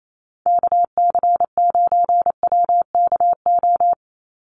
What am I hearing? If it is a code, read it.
Morse code, heard as a keyed tone, about 28 wpm, 700 Hz. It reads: KC9WKO
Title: USA Call Signs 28wpm